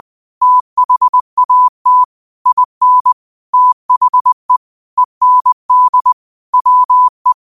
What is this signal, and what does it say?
Morse code, heard as a keyed tone, about 20 wpm, 1000 Hz. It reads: THAT IN THE END WE
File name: qotdfiles / 20wpm